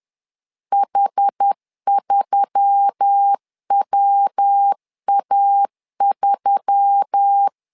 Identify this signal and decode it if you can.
Morse code, heard as a keyed tone, about 10 wpm, 800 Hz. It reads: H3WA3